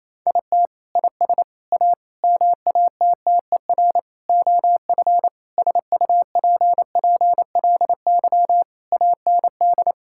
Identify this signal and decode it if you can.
Morse code, heard as a keyed tone, about 27 wpm, 700 Hz. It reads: IT IS A MATTER OF SUPPLY AND
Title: Sentences from Top 1000 Words - Review Part 5 - 28wpm